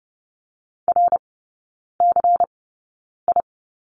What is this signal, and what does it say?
Morse code, heard as a keyed tone, about 30 wpm, spaced as for 9 wpm, 700 Hz. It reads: RCI
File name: Lesson 19e - Letter w - Three Characters 30wpm